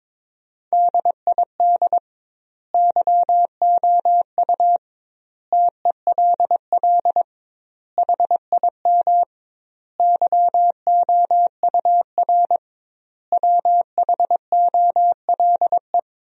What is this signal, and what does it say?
Morse code, heard as a keyed tone, about 22 wpm, 700 Hz. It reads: DID YOU TELL HIM YOUR WHOLE
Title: 2x Word-Spacing - Sentences from Top 400 Words - 22wpm